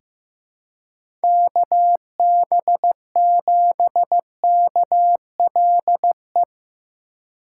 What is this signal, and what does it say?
Morse code, heard as a keyed tone, about 15 wpm, 700 Hz. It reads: KB7KLE
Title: USA Call Signs - 6 Characters - 2x Character Spacing - 15wpm